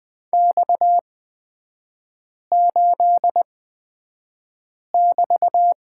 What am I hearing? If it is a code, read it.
Morse code, heard as a keyed tone, about 20 wpm, spaced as for 5 wpm, 700 Hz. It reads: X8<BT>
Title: 255 Lesson 41e - Period - Three Characters 20wpm